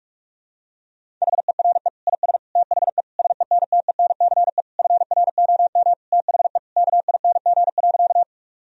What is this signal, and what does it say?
Morse code, heard as a keyed tone, about 46 wpm, 700 Hz. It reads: HERE IS THE SENTENCE FROM THE KING.
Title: Sentences from Top 1000 Words - Review Part 1 - 45wpm